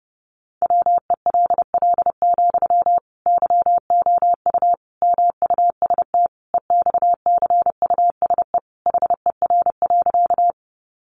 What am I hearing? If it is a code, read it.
Morse code, heard as a keyed tone, about 30 wpm, 700 Hz. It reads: WELL, YOU MUST EXCUSE HER.
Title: Sentences from Top 700 Words 30wpm